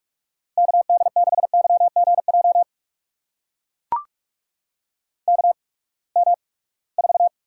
Morse code, heard as a keyed tone, about 44 wpm, spaced as for 25 wpm, 700 Hz. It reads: KD6YGJ KM4
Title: USA Call Signs - 6 Characters - 8x Character Spacing - 45wpm